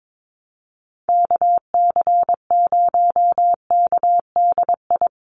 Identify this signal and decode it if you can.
Morse code, heard as a keyed tone, about 22 wpm, 700 Hz. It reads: KC0KDI
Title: USA Call Signs Repeated 3x 22wpm